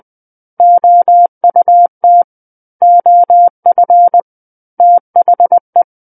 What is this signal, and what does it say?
Morse code, heard as a keyed tone, about 20 wpm, 700 Hz. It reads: OUT OF THE
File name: 20 WPM